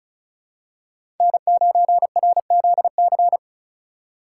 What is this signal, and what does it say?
Morse code, heard as a keyed tone, about 34 wpm, 700 Hz. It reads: N9RZC
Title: USA Call Signs - 5 Characters - 4x Character Spacing - 35wpm